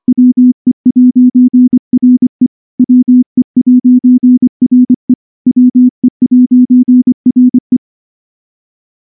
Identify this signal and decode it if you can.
Morse code, heard as a keyed tone, about 23 wpm, 250 Hz. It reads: WE'RE WE'RE WE'RE
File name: morse / morse-training-25WPM